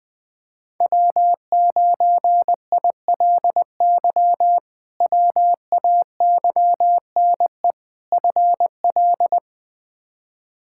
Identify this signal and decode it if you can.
Morse code, heard as a keyed tone, about 20 wpm, 700 Hz. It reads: W9ILY WAYNE FL
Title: K1USN SST QSO Practice - 20wpm